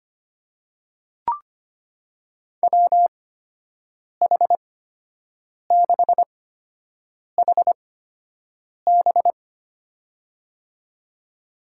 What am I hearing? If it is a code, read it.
Morse code, heard as a keyed tone, about 25 wpm, spaced as for 7 wpm, 700 Hz. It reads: WH6HB
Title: USA Call Signs - 5 Characters - 8x Character Spacing - 25wpm